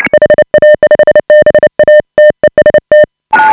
Morse code, heard as a keyed tone, about 29 wpm, 590 Hz. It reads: HA5BA TEST